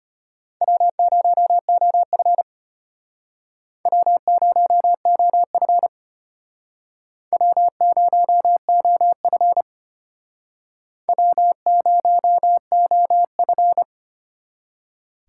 Morse code, heard as a keyed tone, about 33 wpm, 700 Hz. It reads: W0OF W0OF W0OF W0OF